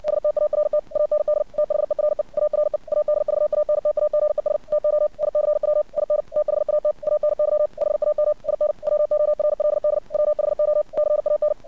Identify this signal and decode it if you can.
Morse code, heard as a keyed tone, about 60 wpm, 600 Hz. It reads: START AND THERE ARE ADVANTAGES TO ICW IN THAT ANY HAM IN WORLD WHO CAN
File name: archived / CRSnet